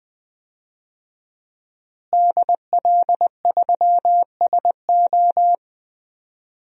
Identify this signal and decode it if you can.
Morse code, heard as a keyed tone, about 20 wpm, 700 Hz. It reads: DL3SO
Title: DXCC-100 Call Signs - 5 Characters - Repeated 3x - 20wpm